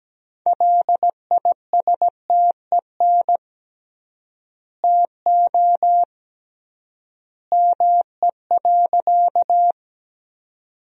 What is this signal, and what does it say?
Morse code, heard as a keyed tone, about 17 wpm, 700 Hz. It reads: LISTEN TO ME.
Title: 3x Word-Spacing - Sentences from Top 400 Words - 17wpm